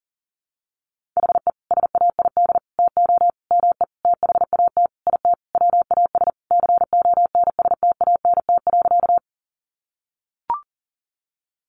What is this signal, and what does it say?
Morse code, heard as a keyed tone, about 40 wpm, 700 Hz. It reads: HE SAID TO ME THAT IT WAS CONSTANT.
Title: Sentences from Top 1500 Words 40wpm